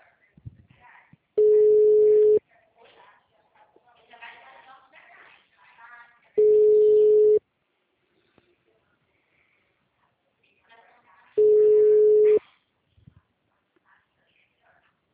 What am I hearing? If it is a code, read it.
Ringback tone (Europe, 425 Hz), with speech.